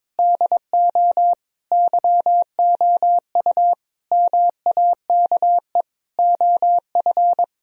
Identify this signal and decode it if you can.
Morse code, heard as a keyed tone, about 21 wpm, 700 Hz. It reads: DO YOU MAKE OF
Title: Sentences from Top 1000 Words - Review Part 1 - 22wpm